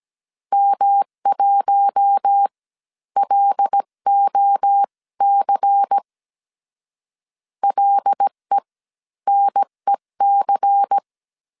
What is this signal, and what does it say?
Morse code, heard as a keyed tone, about 17 wpm, spaced as for 13 wpm, 800 Hz. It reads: M1LOC LENEC
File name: CW / DET / 1 6 1500C S C 16wpm 2733